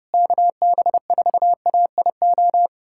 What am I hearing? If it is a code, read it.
Morse code, heard as a keyed tone, about 31 wpm, 700 Hz. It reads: KB4AIO
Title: USA Callsign DE Callsign - 6 Characters - 30wpm